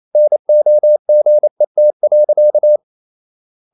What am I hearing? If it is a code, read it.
Morse code, heard as a keyed tone, about 28 wpm, 600 Hz. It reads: NOGET.
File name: Ferie QRQ 28wpm